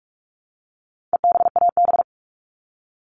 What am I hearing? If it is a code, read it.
Morse code, heard as a keyed tone, about 44 wpm, 700 Hz. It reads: EBAB
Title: All Cut Numbers - 4-Digit 45wpm